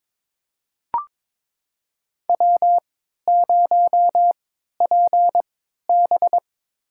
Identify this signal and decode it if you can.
Morse code, heard as a keyed tone, about 22 wpm, spaced as for 13 wpm, 700 Hz. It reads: W0PB
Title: USA Call Signs - 3x Character Spacing - 22wpm